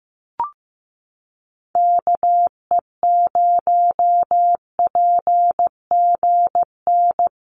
Morse code, heard as a keyed tone, about 15 wpm, 700 Hz. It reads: KE0PGN
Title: USA Call Signs Repeated 3x 15wpm